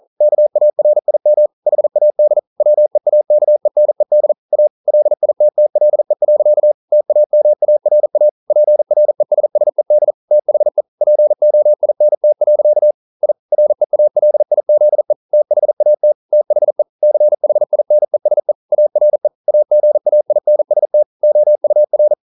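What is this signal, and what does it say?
Morse code, heard as a keyed tone, about 42 wpm, 600 Hz. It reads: KARIM HAD WEAKENED A LITTLE. TAMARA PRESSED THE POINT. I REALIZE THAT THE CHINESE ARE AGAINST OUR
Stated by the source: Morse code at 40 wpm, practice text, CW med Gnister.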